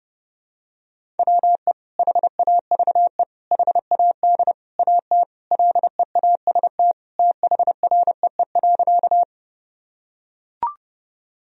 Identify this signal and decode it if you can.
Morse code, heard as a keyed tone, about 30 wpm, 700 Hz. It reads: WE HAVE HAD AT LEAST THREE.
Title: Sentences from Top 1000 Words - Review Part 1 - 30wpm